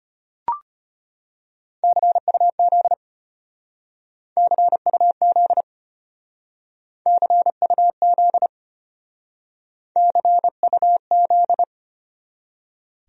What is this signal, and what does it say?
Morse code, heard as a keyed tone, about 32 wpm, 700 Hz. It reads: CUZ CUZ CUZ CUZ